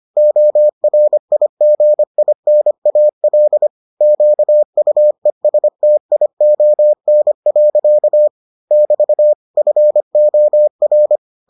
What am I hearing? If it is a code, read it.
Morse code, heard as a keyed tone, about 25 wpm, 600 Hz. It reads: ORIGINAL QUESTION. <BT> FOR